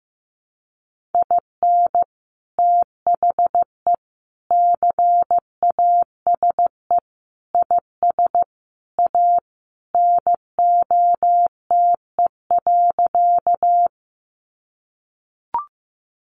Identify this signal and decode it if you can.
Morse code, heard as a keyed tone, about 15 wpm, 700 Hz. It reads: IN THE CASE IS A NOTE.
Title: Sentences from Top 700 Words 15wpm